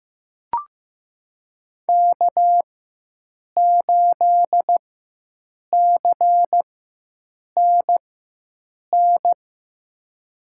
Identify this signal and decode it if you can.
Morse code, heard as a keyed tone, about 15 wpm, spaced as for 7 wpm, 700 Hz. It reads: K8CNN
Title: USA Call Signs - 4x Character Spacing - 15wpm